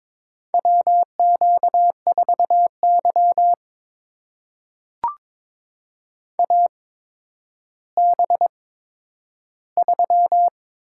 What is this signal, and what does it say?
Morse code, heard as a keyed tone, about 21 wpm, spaced as for 11 wpm, 700 Hz. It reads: WQ4Y AB3
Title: USA Call Signs - 8x Character Spacing - 22wpm